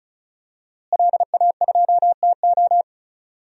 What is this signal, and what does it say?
Morse code, heard as a keyed tone, about 36 wpm, 700 Hz. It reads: LA2TO
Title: DXCC-100 Call Signs - 5 Characters - Repeated 2x - 35wpm